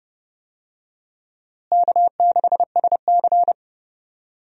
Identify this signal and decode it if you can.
Morse code, heard as a keyed tone, about 32 wpm, 700 Hz. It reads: K6SC
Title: USA Call Signs - 4 Characters - Repeated 2x - 30wpm